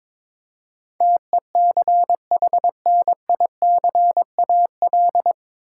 Morse code, heard as a keyed tone, about 22 wpm, 700 Hz. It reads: TECHNICAL
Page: Morse Code Ninja practice 22wpm